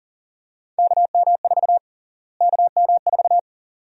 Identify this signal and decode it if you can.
Morse code, heard as a keyed tone, about 41 wpm, 700 Hz. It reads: KM4 KM4
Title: USA Call Sign Prefixes - Repeated 2x - 40wpm